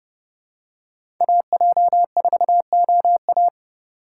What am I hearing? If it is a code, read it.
Morse code, heard as a keyed tone, about 30 wpm, 700 Hz. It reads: AJ4OA
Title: USA Call Signs - 5 Characters - 3x Character Spacing - 30wpm